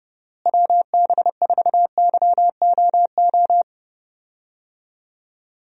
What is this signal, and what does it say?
Morse code, heard as a keyed tone, about 30 wpm, 700 Hz. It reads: WB4YOO
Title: USA Call Signs - 6 Characters - Repeated 3x - 30wpm